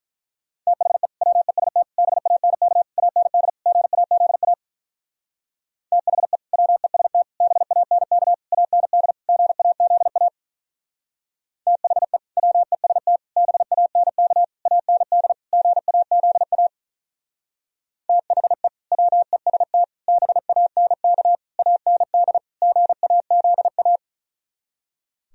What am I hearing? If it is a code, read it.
Morse code, heard as a keyed tone, about 44 wpm, 700 Hz. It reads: THE WEST BANK AND GAZA THE WEST BANK AND GAZA THE WEST BANK AND GAZA THE WEST BANK AND GAZA